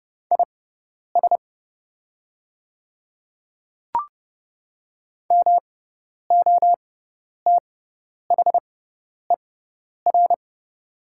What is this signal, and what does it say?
Morse code, heard as a keyed tone, about 30 wpm, spaced as for 10 wpm, 700 Hz. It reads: IS MOTHER
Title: 6x Character Spacing - Top 300 Words - 30wpm